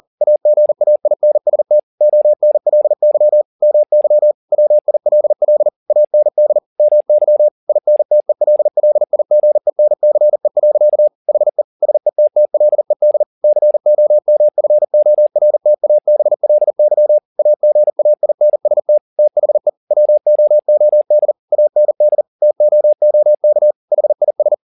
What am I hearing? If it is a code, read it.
Morse code, heard as a keyed tone, about 40 wpm, 600 Hz. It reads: AGAINST ONLY MY WILL AND MY INTELLIGENCE. HE SETTLED COMFORTABLY AGAINST THE WOOD AND TOOK HIS